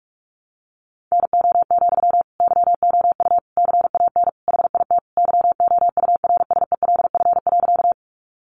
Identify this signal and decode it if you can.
Morse code, heard as a keyed tone, about 44 wpm, 700 Hz. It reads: NO, YOU CAN HIT YOURSELF.
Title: Sentences from Top 1200 Words 45wpm